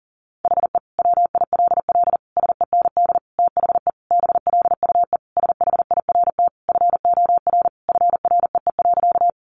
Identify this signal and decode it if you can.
Morse code, heard as a keyed tone, about 40 wpm, 700 Hz. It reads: HE WILL SEND THE BLUE SHIRT FOR FREE.
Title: Additional Easy Phrases 40wpm